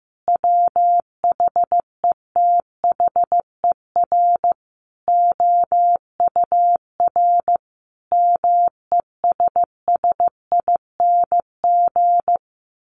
Morse code, heard as a keyed tone, about 15 wpm, 700 Hz. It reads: WHETHER OUR MESSING
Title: A Sound Of Thunder 15wpm